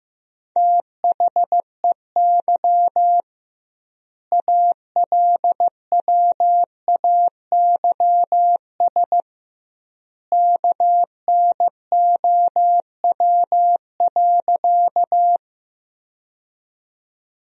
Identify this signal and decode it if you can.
Morse code, heard as a keyed tone, about 16 wpm, 700 Hz. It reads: THEY ALWAYS KNOW.
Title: 2x Word-Spacing - Sentences from Top 300 Words - 15wpm